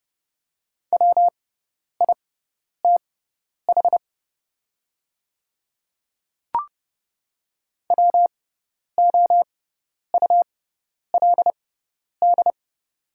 Morse code, heard as a keyed tone, about 30 wpm, spaced as for 10 wpm, 690 Hz. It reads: WITH WOULD
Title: 6x Character Spacing - Top 300 Words - 30wpm